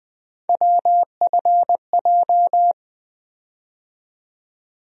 Morse code, heard as a keyed tone, about 20 wpm, 700 Hz. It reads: WFJ
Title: USA Call Sign Postfixes - 3 Letters 20wpm